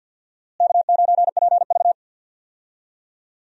Morse code, heard as a keyed tone, about 50 wpm, 700 Hz. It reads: K9PV
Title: USA Call Signs - 4 Characters - Repeated 2x - 50wpm